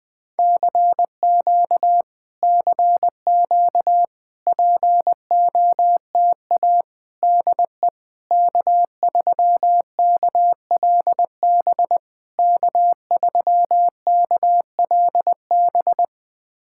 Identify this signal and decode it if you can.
Morse code, heard as a keyed tone, about 20 wpm, 700 Hz. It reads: CQ CQ POTA DE K3KLB K3KLB
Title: POTA QSO Practice - 20wpm